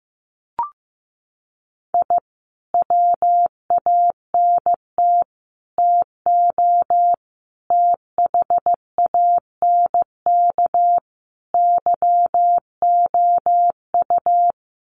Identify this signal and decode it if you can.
Morse code, heard as a keyed tone, about 15 wpm, 700 Hz. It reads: I WANT TO THANK YOU
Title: Sets of 5 Words Encore 15wpm